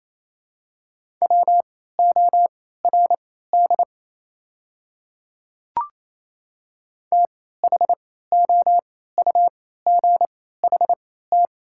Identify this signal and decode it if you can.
Morse code, heard as a keyed tone, about 28 wpm, spaced as for 16 wpm, 700 Hz. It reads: WORD THOUGHT
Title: 3x Character Spacing - Top 300 Words - 28wpm